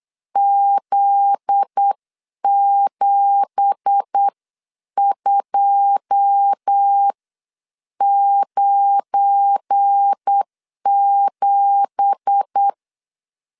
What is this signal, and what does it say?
Morse code, heard as a keyed tone, about 8 wpm, spaced as for 7 wpm, 800 Hz. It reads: Z7297
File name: CW / DET / 1 9 1500C S B 8wpm 5624